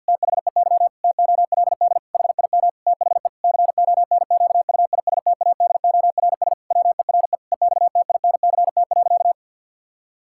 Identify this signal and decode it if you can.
Morse code, heard as a keyed tone, about 50 wpm, 700 Hz. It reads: THEY TOLD HIM THE CONQUISTADORS WERE EXTINCT.
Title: Pangram Challenge v2 - 50wpm